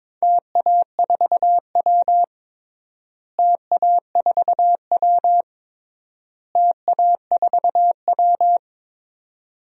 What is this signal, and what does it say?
Morse code, heard as a keyed tone, about 22 wpm, 700 Hz. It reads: TA4W TA4W TA4W
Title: DXCC-100 Call Signs - 4 Characters - Repeated 3x - 22wpm